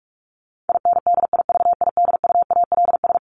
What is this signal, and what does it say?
Morse code, heard as a keyed tone, about 44 wpm, 700 Hz. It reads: INDIVIDUALS
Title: ES:S Suffix - Top 100 Words with Base Words 45wpm